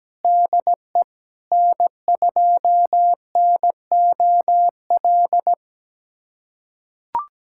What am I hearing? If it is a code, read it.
Morse code, heard as a keyed tone, about 17 wpm, 700 Hz. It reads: DE N2NOL
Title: USA Callsign DE Callsign 17wpm